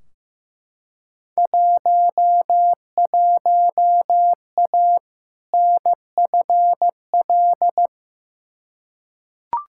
Morse code, heard as a keyed tone, about 15 wpm, 700 Hz. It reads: 11A NFL
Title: Field Day Exchanges Encore - 15wpm